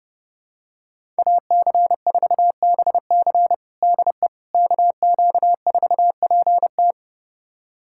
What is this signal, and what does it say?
Morse code, heard as a keyed tone, about 30 wpm, 700 Hz. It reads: AC4BC DE KQ4PT
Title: USA Callsign DE Callsign - 5 Characters - 30wpm